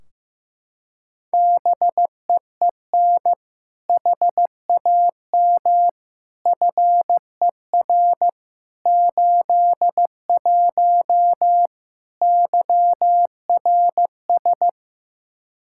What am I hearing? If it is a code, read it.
Morse code, heard as a keyed tone, about 16 wpm, 700 Hz. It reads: BEEN HAM FER 81 YRS
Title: Practice QSO - Exchange 3 - More Info - 15wpm